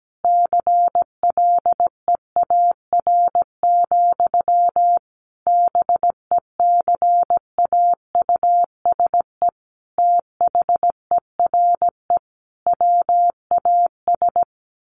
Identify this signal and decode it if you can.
Morse code, heard as a keyed tone, about 17 wpm, 700 Hz. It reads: CLEAR, BECAUSE THERE WAS
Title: Sentences from Top 1000 Words 17wpm